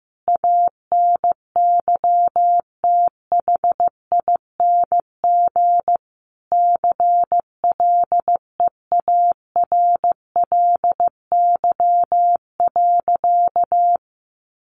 Morse code, heard as a keyed tone, about 15 wpm, 700 Hz. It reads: ANYTHING CLEARLY.
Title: Sentences from Top 1300 Words 15wpm